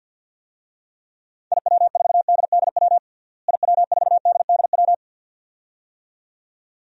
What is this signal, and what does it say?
Morse code, heard as a keyed tone, about 49 wpm, 700 Hz. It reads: IW4DDW IW4DDW
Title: DXCC-100 Call Signs - 6 Characters - Repeated 2x - 50wpm